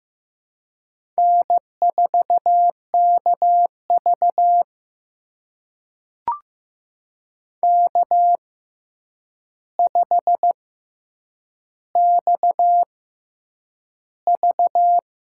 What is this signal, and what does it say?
Morse code, heard as a keyed tone, about 15 wpm, spaced as for 8 wpm, 700 Hz. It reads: N4KV K5XV
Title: USA Call Signs - 4 Characters - 6x Character Spacing - 15wpm